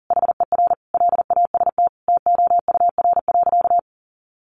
Morse code, heard as a keyed tone, about 40 wpm, 700 Hz. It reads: HER LAST TOUR.
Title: Sentences from Top 1400 Words 40wpm